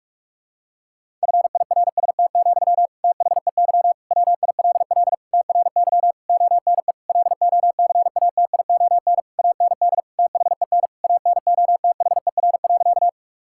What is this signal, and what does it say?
Morse code, heard as a keyed tone, about 41 wpm, 700 Hz. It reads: FIRST, THEY WILL TRY ONE LOCATION AND THEN ANOTHER.
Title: Sentences from Top 1000 Words - Review Part 6 - 45wpm